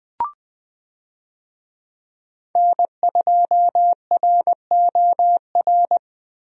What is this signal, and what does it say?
Morse code, heard as a keyed tone, about 20 wpm, 700 Hz. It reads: N2ROR
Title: USA Call Signs - 5 Characters - Repeated 2x - 20wpm